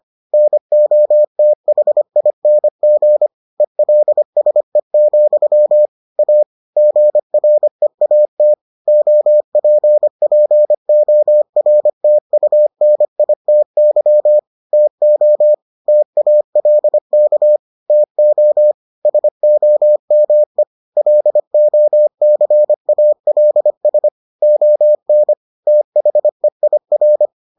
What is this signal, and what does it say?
Morse code, heard as a keyed tone, about 25 wpm, 600 Hz. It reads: NOTHING ELSE, A GREAT OPPORTUNITY TO TALK TO SOME LOCALS ON THEIR